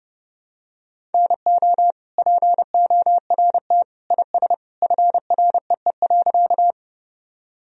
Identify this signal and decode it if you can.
Morse code, heard as a keyed tone, about 30 wpm, 700 Hz. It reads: NO PORT IS FREE.
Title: Sentences from Top 1000 Words - Review Part 1 - 30wpm